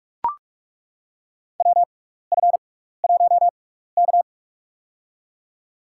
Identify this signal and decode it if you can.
Morse code, heard as a keyed tone, about 41 wpm, spaced as for 15 wpm, 700 Hz. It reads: WF1K
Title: USA Call Signs - 4 Characters - 6x Character Spacing - 45wpm